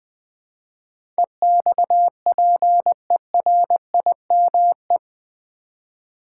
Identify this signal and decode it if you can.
Morse code, heard as a keyed tone, about 20 wpm, 700 Hz. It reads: EXPERIME
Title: Build A Word - 4 Letters At A Time - 10-Letter Words - 20wpm